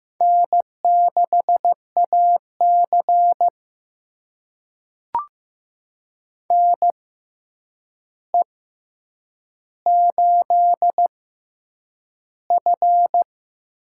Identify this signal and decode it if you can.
Morse code, heard as a keyed tone, about 15 wpm, spaced as for 8 wpm, 700 Hz. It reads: N6AC NE8F
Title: USA Call Signs - 4 Characters - 6x Character Spacing - 15wpm